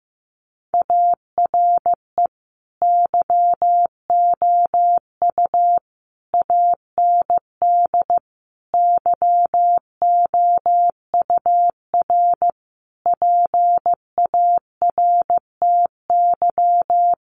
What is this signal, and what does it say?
Morse code, heard as a keyed tone, about 15 wpm, 700 Hz. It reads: ARE YOU AND YOUR PARTY
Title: Sentences from Top 1400 Words 15wpm